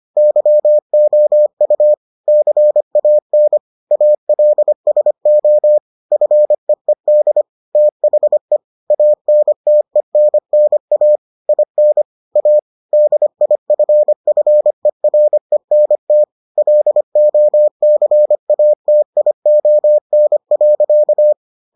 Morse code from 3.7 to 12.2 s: ALSO FEED THE ANTENNA IN

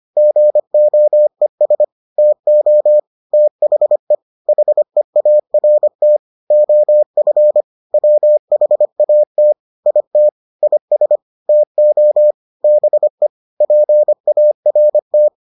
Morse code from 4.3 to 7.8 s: HEART OF

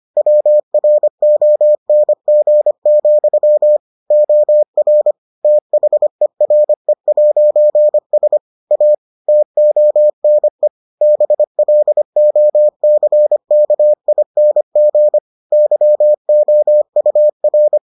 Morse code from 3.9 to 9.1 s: OR THERE'S A

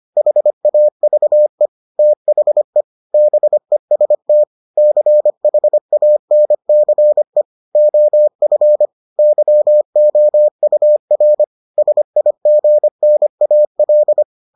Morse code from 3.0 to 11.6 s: BEST CHANCE OF YOUR

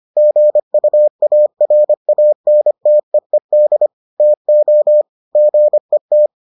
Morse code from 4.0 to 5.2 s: TO

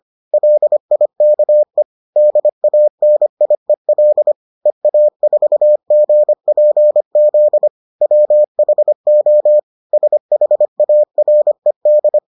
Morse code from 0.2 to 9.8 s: LIKE DANIEL EA4GPZ WHO

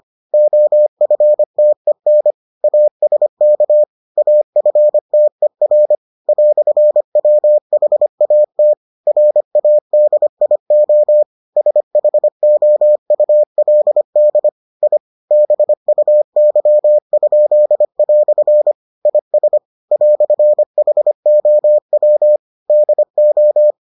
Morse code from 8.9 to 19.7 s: RADIO SHOULD I BUY?" IS

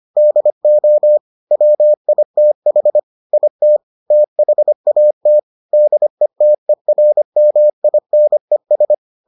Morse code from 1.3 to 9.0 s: WITH IT THAT DETERMINES

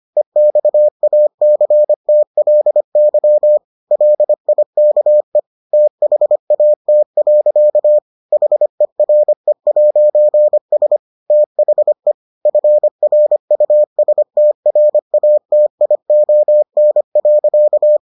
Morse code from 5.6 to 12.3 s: THAT. HERE'S THE

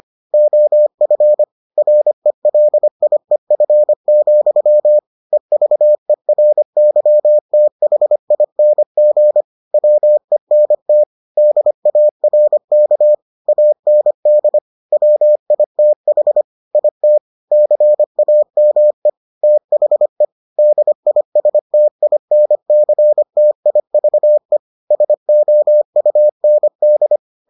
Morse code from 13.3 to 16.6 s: AND WITH

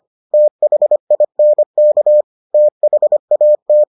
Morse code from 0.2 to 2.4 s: THINK

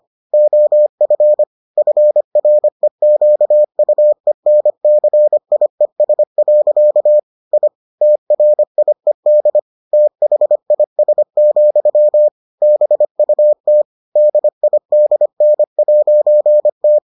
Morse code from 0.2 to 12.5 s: OF FREQUENCIES. I TRIED THIS,

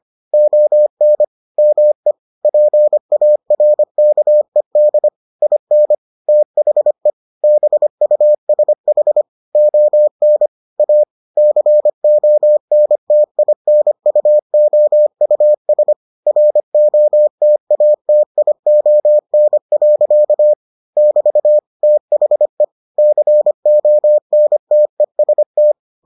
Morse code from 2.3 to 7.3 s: PARKED IN THE

